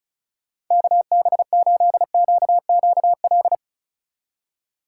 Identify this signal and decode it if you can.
Morse code, heard as a keyed tone, about 46 wpm, 700 Hz. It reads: KB8QQL